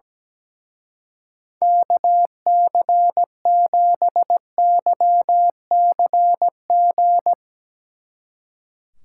Morse code, heard as a keyed tone, about 18 wpm, 700 Hz. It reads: KC7YCG
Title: Speed Racing - USA 2x3 Call Signs 17wpm